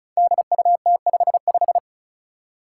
Morse code, heard as a keyed tone, about 36 wpm, 700 Hz. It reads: DUT55
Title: Five Challenging Letter-Number Combinations 35wpm